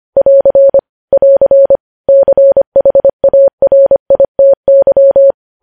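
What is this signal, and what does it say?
Morse code, heard as a keyed tone, about 25 wpm, 550 Hz. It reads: <AR> <AR> CHARITY
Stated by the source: Morse code news headlines at 25 wpm.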